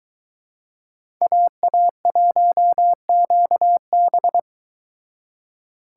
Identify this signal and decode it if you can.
Morse code, heard as a keyed tone, about 22 wpm, 700 Hz. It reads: AA1QB